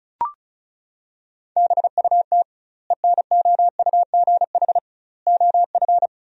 Morse code, heard as a keyed tone, about 36 wpm, 700 Hz. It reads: BUT ENOUGH OF
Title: 2x Word-Spacing - Sentences from Top 400 Words - 35wpm